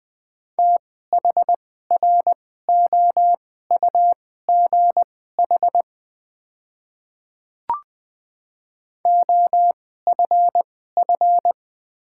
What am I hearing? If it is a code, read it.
Morse code, heard as a keyed tone, about 20 wpm, spaced as for 15 wpm, 700 Hz. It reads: THROUGH OFF
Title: 2x Character Spacing - Top 200 Words - 20wpm